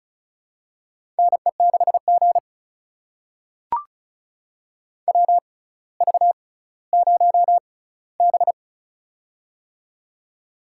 Morse code, heard as a keyed tone, about 36 wpm, spaced as for 18 wpm, 700 Hz. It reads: NE6G WV0B
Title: USA Call Signs - 4 Characters - 6x Character Spacing - 35wpm